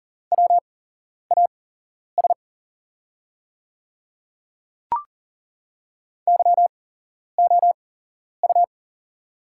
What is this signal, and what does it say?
Morse code, heard as a keyed tone, about 40 wpm, spaced as for 11 wpm, 690 Hz. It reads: WAS YOU
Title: 8x Character Spacing - Top 100 Words - 40wpm